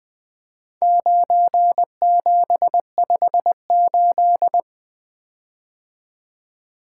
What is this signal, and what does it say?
Morse code, heard as a keyed tone, about 21 wpm, 700 Hz. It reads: 9758
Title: Four Letter-Number Combinations - 20wpm